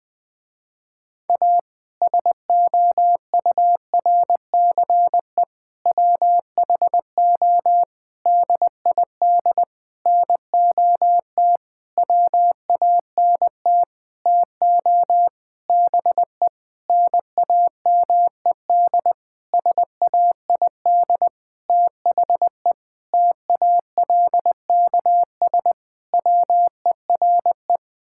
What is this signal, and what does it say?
Morse code, heard as a keyed tone, about 20 wpm, 700 Hz. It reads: A SOURCE WHO DID NOT WANT TO BE NAMED SAID THE TALKS WERE
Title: Sentences with 8-Word Phrases - Spoken Beginning - 20wpm